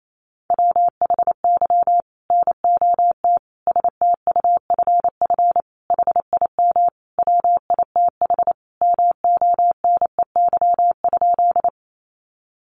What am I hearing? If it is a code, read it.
Morse code, heard as a keyed tone, about 28 wpm, 700 Hz. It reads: WHY NOT STUFF HIM WITH MONEY?
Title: Sentences from Top 1400 Words 28wpm